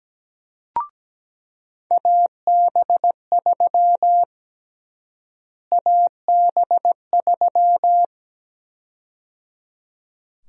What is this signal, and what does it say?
Morse code, heard as a keyed tone, about 17 wpm, 700 Hz. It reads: AB3 AB3
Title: USA Call Sign Prefixes - Repeated 2x - 17wpm